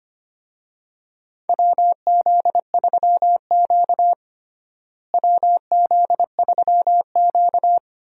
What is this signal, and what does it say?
Morse code, heard as a keyed tone, about 25 wpm, 700 Hz. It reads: WZ3Q WZ3Q
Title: USA Call Signs - 4 Characters - Repeated 2x - 25wpm